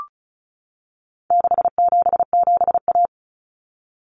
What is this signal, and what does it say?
Morse code, heard as a keyed tone, about 34 wpm, 700 Hz. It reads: 677A
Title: Common Cut Numbers - 4-Digit 35wpm